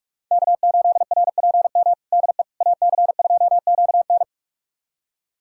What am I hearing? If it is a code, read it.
Morse code, heard as a keyed tone, about 46 wpm, 700 Hz. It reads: K8RPM DE AC2QN
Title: USA Callsign DE Callsign - 5 Characters - 45wpm